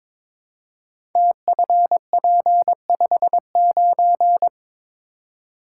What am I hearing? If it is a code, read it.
Morse code, heard as a keyed tone, about 23 wpm, 700 Hz. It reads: TFP59
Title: Code Groups - 5 Letter:Numbers - 22wpm